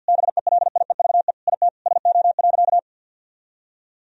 Morse code, heard as a keyed tone, about 50 wpm, 700 Hz. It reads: BELIEVE IT SO.
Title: Sentences from Top 1000 Words - Review Part 4 - 50wpm